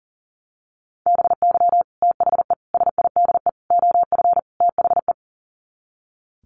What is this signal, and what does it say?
Morse code, heard as a keyed tone, about 40 wpm, 700 Hz. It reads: BY THE SIDE OF THE
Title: Sets of 5 Words Encore2 40wpm